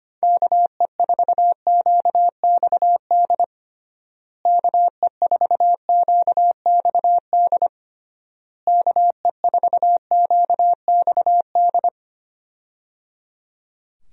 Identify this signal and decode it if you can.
Morse code, heard as a keyed tone, about 25 wpm, 700 Hz. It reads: KE4QXD KE4QXD KE4QXD
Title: USA Call Signs - 6 Characters - Repeated 3x - 25wpm